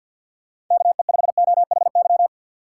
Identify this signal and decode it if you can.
Morse code, heard as a keyed tone, about 53 wpm, 700 Hz. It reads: KE5OHY
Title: USA Call Signs - 6 Characters - 8x Character Spacing - 50wpm